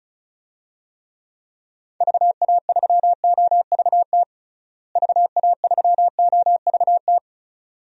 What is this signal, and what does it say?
Morse code, heard as a keyed tone, about 36 wpm, 700 Hz. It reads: VA3OVT VA3OVT
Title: DXCC-100 Call Signs - 6 Characters - Repeated 3x - 35wpm